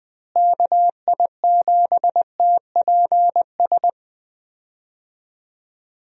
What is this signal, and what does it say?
Morse code, heard as a keyed tone, about 20 wpm, 700 Hz. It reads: KI7TPS
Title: USA Call Signs - 6 Characters - Repeated 2x - 20wpm